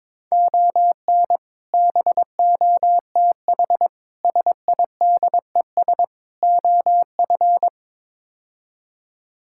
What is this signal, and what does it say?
Morse code, heard as a keyed tone, about 21 wpm, 700 Hz. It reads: ON BOTH SIDES OF
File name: practice / 22wpm / Build A Phrase - 5-Word Phrases - 22wpm